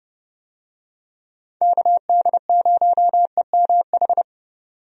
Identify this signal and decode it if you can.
Morse code, heard as a keyed tone, about 32 wpm, 700 Hz. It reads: KD0EMH
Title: USA Call Signs - 6 Characters - 30wpm